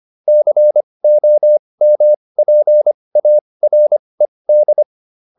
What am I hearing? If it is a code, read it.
Morse code, heard as a keyed tone, about 25 wpm, spaced as for 20 wpm, 600 Hz. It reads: COMPARED